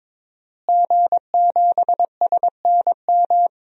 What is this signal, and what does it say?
Morse code, heard as a keyed tone, about 22 wpm, 700 Hz. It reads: G7SNM
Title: DXCC-100 Call Signs - 5 Characters - Repeated 2x - 22wpm